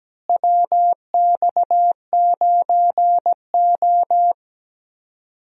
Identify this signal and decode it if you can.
Morse code, heard as a keyed tone, about 17 wpm, 700 Hz. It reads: WX9O
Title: USA Call Signs - 4 Characters - Repeated 2x - 17wpm